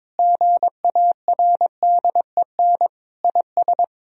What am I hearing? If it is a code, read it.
Morse code, heard as a keyed tone, about 22 wpm, 700 Hz. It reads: GARDEN IS
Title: Sentences from Top 1000 Words - Review Part 4 - 22wpm